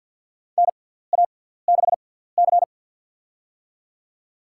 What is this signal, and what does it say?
Morse code, heard as a keyed tone, about 51 wpm, spaced as for 17 wpm, 700 Hz. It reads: NA6C
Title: USA Call Signs - 4 Characters - 6x Character Spacing - 50wpm